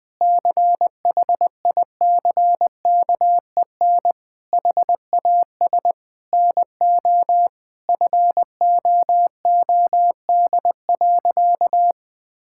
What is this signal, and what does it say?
Morse code, heard as a keyed tone, about 20 wpm, 700 Hz. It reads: CHICKEN HAS NO FOOD.
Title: Sentences from Top 1000 Words - Review Part 5 - 20wpm